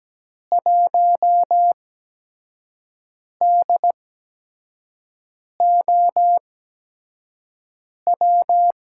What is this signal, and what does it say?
Morse code, heard as a keyed tone, about 17 wpm, spaced as for 5 wpm, 700 Hz. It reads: 1DOW
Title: USA Call Signs - 6 Characters - 8x Character Spacing - 17wpm